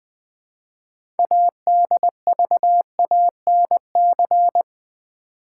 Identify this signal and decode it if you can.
Morse code, heard as a keyed tone, about 20 wpm, 700 Hz. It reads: ADVANC
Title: Build A Word - 3 Letters At A Time - 8-Letter Words - 20wpm